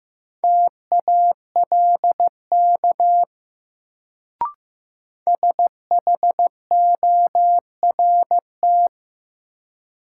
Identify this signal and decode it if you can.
Morse code, heard as a keyed tone, about 16 wpm, 700 Hz. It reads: TALK SHORT
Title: Sentences from Top 400 Words - New Word Drill - 15wpm